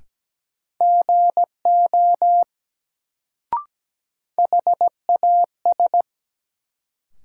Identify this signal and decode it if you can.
Morse code, heard as a keyed tone, about 17 wpm, 700 Hz. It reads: GO HAS
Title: Sentences from Top 100 Words - New Word Drill - 17wpm